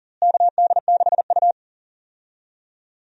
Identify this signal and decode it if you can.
Morse code, heard as a keyed tone, about 37 wpm, 700 Hz. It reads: KD6U
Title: USA Call Signs - 4 Characters - Repeated 3x - 40wpm